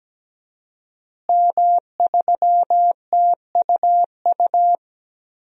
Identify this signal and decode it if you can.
Morse code, heard as a keyed tone, about 17 wpm, 700 Hz. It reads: M3TUU
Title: DXCC-100 Call Signs - 5 Characters - Repeated 2x - 17wpm